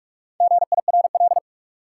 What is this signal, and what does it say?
Morse code, heard as a keyed tone, about 48 wpm, 700 Hz. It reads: GIRL
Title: Four Letter Words - 45wpm